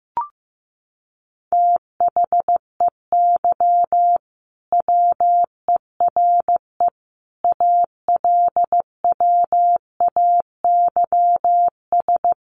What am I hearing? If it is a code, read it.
Morse code, heard as a keyed tone, about 14 wpm, 700 Hz. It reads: THEY WERE ALWAYS
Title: Sentences from Top 1400 Words 15wpm